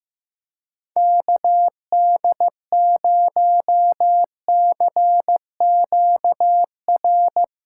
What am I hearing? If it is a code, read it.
Morse code, heard as a keyed tone, about 15 wpm, 700 Hz. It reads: KD0CQR
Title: USA Call Signs - 6 Characters - 3x Character Spacing - 15wpm